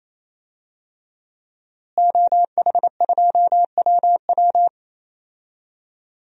Morse code, heard as a keyed tone, about 28 wpm, 700 Hz. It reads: OH2WW
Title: DXCC-100 Call Signs - 5 Characters - Repeated 3x - 28wpm